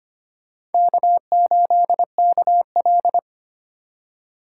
Morse code, heard as a keyed tone, about 32 wpm, 700 Hz. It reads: K8KL